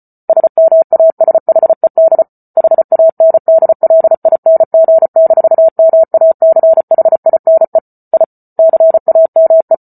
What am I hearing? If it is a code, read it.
Morse code, heard as a keyed tone, about 34 wpm, 650 Hz. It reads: SMASHED HANDLING-MACHINE I CAME